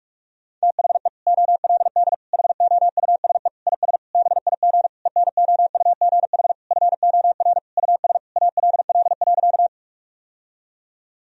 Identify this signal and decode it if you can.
Morse code, heard as a keyed tone, about 42 wpm, 700 Hz. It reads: THE OLD HOUSE IS BIG ENOUGH FOR US ALL.
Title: Sentences from Top 1000 Words - Review Part 1 - 45wpm